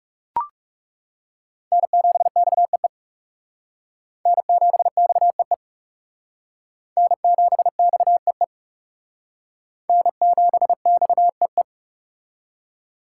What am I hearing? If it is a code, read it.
Morse code, heard as a keyed tone, about 35 wpm, 700 Hz. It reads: N7XEE N7XEE N7XEE N7XEE
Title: Speed Racing - USA 1x3 Call Signs 30wpm